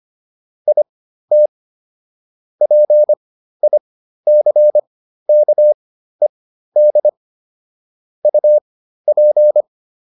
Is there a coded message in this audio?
Yes: Morse code from 0.7 to 9.6 s, heard as a keyed tone, about 25 wpm, spaced as for 13 wpm, 600 Hz. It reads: IT PICKED UP